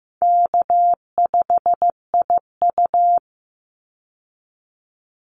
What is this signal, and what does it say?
Morse code, heard as a keyed tone, about 15 wpm, 700 Hz. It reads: K5IU
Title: Easy Call Signs Repeated 3x - 15wpm